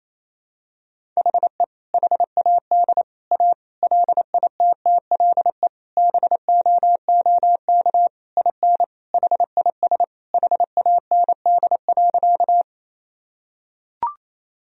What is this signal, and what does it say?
Morse code, heard as a keyed tone, about 28 wpm, 700 Hz. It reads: HE HAD A LITTLE BOOK IN HIS HAND.
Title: Sentences from Top 1000 Words - Review Part 1 - 28wpm